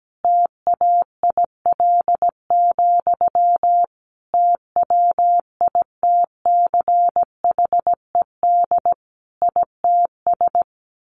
Morse code, heard as a keyed tone, about 17 wpm, 700 Hz. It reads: TAIL, TWITCHED ITS
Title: A Sound Of Thunder 17wpm